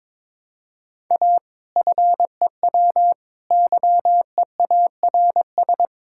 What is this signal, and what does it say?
Morse code, heard as a keyed tone, about 22 wpm, 700 Hz. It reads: A FEW YEARS
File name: practice / 22wpm / Sentences with 4-Word Phrases - 22wpm